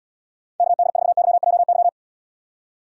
Morse code, heard as a keyed tone, about 78 wpm, 700 Hz. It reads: KD4YJY